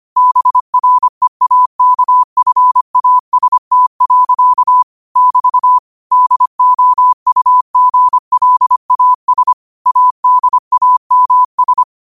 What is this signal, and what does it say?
Morse code, heard as a keyed tone, about 25 wpm, 1000 Hz. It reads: BREAKFAST. <BT> DOUGLAS ADAMS
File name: qotdfiles / 25wpm